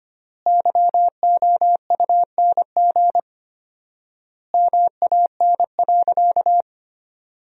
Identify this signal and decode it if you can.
Morse code, heard as a keyed tone, about 25 wpm, 700 Hz. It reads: YOUNG MAN.
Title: 4x Word-Spacing - Sentences from Top 500 Words - 25wpm